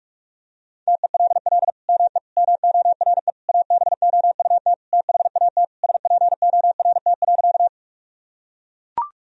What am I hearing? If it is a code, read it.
Morse code, heard as a keyed tone, about 41 wpm, 700 Hz. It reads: TELL ME MORE ABOUT THAT SPORT.
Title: Sentences from Top 1000 Words - Review Part 4 - 45wpm